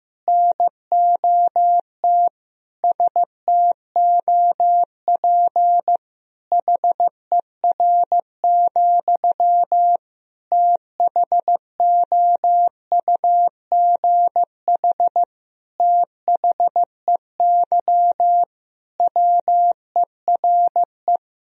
Morse code, heard as a keyed tone, about 15 wpm, 700 Hz. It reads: NOT STOP HER, THOUGH THEY WERE
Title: Sentences from Top 1000 Words - Review Part 2 - 15wpm